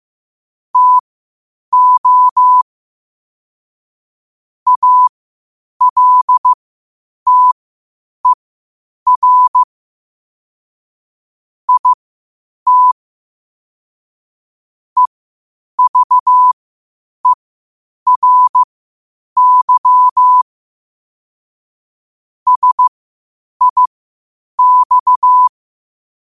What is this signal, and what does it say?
Morse code, heard as a keyed tone, about 15 wpm, spaced as for 8 wpm, 1000 Hz. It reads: TO ALTER IT EVERY SIX
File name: qotdfiles / 8wpm